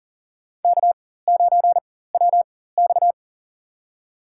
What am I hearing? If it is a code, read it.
Morse code, heard as a keyed tone, about 40 wpm, spaced as for 19 wpm, 700 Hz. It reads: K9WX
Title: USA Call Signs - 4 Characters - 4x Character Spacing - 40wpm